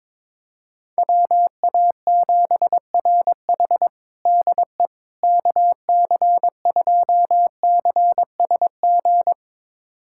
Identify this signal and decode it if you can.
Morse code, heard as a keyed tone, about 22 wpm, 700 Hz. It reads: WA7RH DE KC2CSG
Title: USA Callsign DE Callsign 22wpm